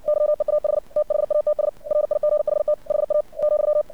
Morse code, heard as a keyed tone, about 59 wpm, 600 Hz. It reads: YEAH THATS RIGHT HA ,
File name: archived / CRSnet